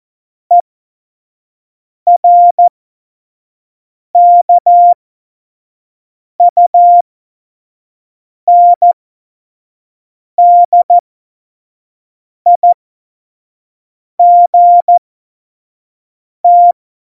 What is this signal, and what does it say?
Morse code, heard as a keyed tone, about 14 wpm, spaced as for 5 wpm, 700 Hz. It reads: ERKUNDIGT
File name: Morsekurs / Witze / Witz 29. September 2020, 25 F70